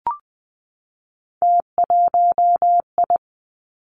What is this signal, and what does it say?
Morse code, heard as a keyed tone, about 20 wpm, 700 Hz. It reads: T1I
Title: Three Challenging Letter-Number Combinations 20wpm